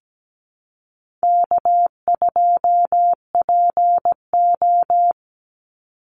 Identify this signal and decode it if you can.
Morse code, heard as a keyed tone, about 17 wpm, 700 Hz. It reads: K2PO
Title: Easy Call Signs Repeated 2x - 17wpm